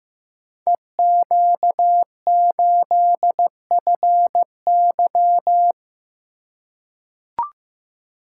Morse code, heard as a keyed tone, about 15 wpm, 700 Hz. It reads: EQ8FY
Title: Code Groups - 5 Letter:Numbers - 15wpm